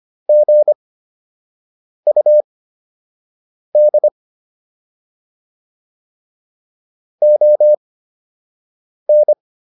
Morse code, heard as a keyed tone, about 25 wpm, spaced as for 6 wpm, 600 Hz. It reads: GUD ON